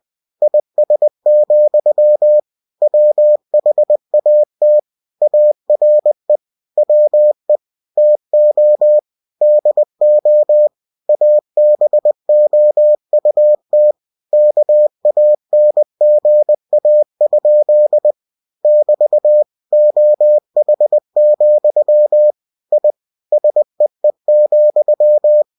Morse code, heard as a keyed tone, about 20 wpm, 600 Hz. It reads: IS, WHAT ARE WE TO DO ABOUT KANGA? <BT> OH, I SEE,
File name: cw-books / winnie-the-pooh / winnie-20wpm-page-0061